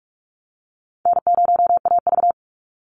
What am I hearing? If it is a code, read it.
Morse code, heard as a keyed tone, about 44 wpm, 700 Hz. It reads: N0AV
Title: Easy Call Signs - 45wpm